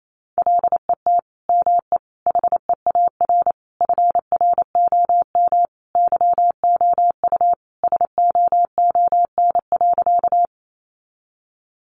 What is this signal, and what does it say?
Morse code, heard as a keyed tone, about 28 wpm, 700 Hz. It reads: LET ME HEAR FROM YOU SOON.
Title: Sentences from Top 400 Words 28wpm